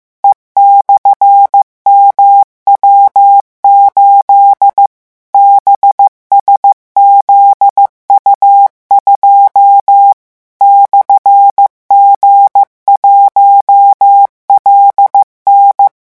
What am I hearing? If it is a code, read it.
Morse code, heard as a keyed tone, about 15 wpm, 800 Hz. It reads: E/MW8 BSZU2 /G1LN
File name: morse mp3 / revision12-vitesse 14 mots